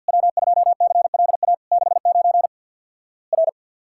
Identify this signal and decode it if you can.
Morse code, heard as a keyed tone, about 50 wpm, 700 Hz. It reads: W2CLA 69 R
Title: Worked All Europe DX Contest - QTC Exchange with Repeats - 50wpm